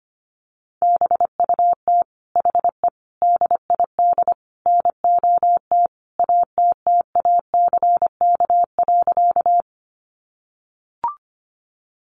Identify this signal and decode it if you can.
Morse code, heard as a keyed tone, about 25 wpm, 700 Hz. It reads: BUT HE DID NOT ATTACK.
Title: Sentences from Top 1500 Words 25wpm